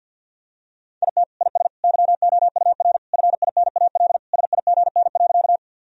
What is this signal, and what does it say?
Morse code, heard as a keyed tone, about 52 wpm, 700 Hz. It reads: IT IS YOUR FINAL SIGN.
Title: Sentences from Top 1000 Words - Review Part 2 - 50wpm